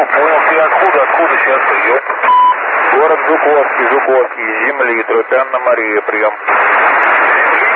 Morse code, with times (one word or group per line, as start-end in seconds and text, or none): none